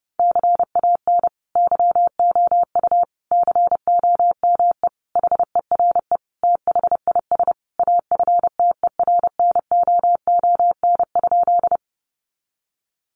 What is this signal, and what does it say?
Morse code, heard as a keyed tone, about 30 wpm, 700 Hz. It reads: CAN YOU COME HERE THIS AFTERNOON?
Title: Sentences from Top 1500 Words 30wpm